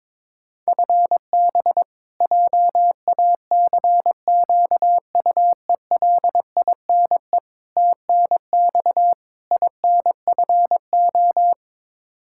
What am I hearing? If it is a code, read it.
Morse code, heard as a keyed tone, about 22 wpm, 700 Hz. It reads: FB JACQUELINE TNX INFO
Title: Practice QSO - Exchange 3 - More Info - 22wpm